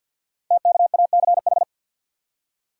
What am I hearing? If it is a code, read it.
Morse code, heard as a keyed tone, about 50 wpm, 690 Hz. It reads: TKACH
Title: Code Groups - 5 Letters - 50wpm